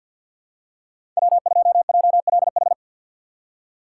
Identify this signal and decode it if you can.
Morse code, heard as a keyed tone, about 47 wpm, 700 Hz. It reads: W2JLH
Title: USA Call Signs - 5 Characters - 2x Character Spacing - 50wpm